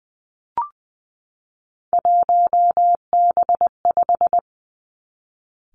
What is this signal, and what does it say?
Morse code, heard as a keyed tone, about 19 wpm, 700 Hz. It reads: 1B5
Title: Three Challenging Letter-Number Combinations 20wpm